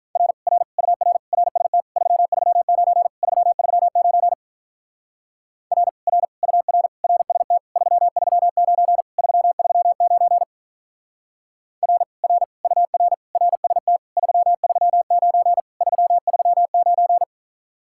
Morse code, heard as a keyed tone, about 48 wpm, 700 Hz. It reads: R R UR RST 339 339 R R UR RST 339 339 R R UR RST 339 339